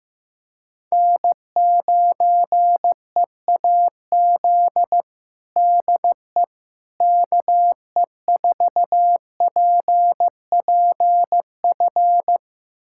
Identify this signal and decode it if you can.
Morse code, heard as a keyed tone, about 15 wpm, 700 Hz. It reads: N9EAZ DE KE4PPF
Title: USA Callsign DE Callsign 15wpm